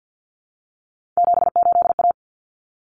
Morse code, heard as a keyed tone, about 50 wpm, 700 Hz. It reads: 78A
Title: Common Cut Numbers - 3-Digit 50wpm